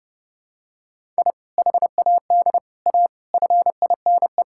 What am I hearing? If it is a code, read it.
Morse code, heard as a keyed tone, about 30 wpm, 700 Hz. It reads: I HAD A FINE
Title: Sentences from Top 1000 Words - Review Part 6 - 30wpm